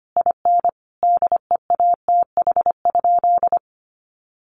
Morse code, heard as a keyed tone, about 25 wpm, 700 Hz. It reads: IN DEATH?
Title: Sentences from Top 1300 Words 25wpm